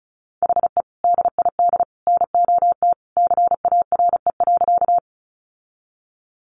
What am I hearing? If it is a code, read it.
Morse code, heard as a keyed tone, about 34 wpm, 700 Hz. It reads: HE DID NOT CARE.
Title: Sentences from Top 600 Words 35wpm